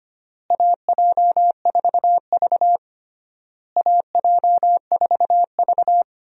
Morse code, heard as a keyed tone, about 25 wpm, 700 Hz. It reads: AJ4V AJ4V
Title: USA Call Signs - 4 Characters - Repeated 3x - 25wpm